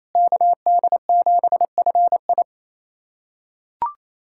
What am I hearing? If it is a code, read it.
Morse code, heard as a keyed tone, about 28 wpm, 700 Hz. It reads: KD7FI
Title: USA Callsign DE Callsign - 5 Characters - 28wpm